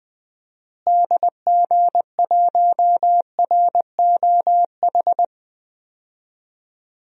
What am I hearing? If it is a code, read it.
Morse code, heard as a keyed tone, about 20 wpm, 700 Hz. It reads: DG1ROH
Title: DXCC-100 Call Signs - 6 Characters - Repeated 2x - 20wpm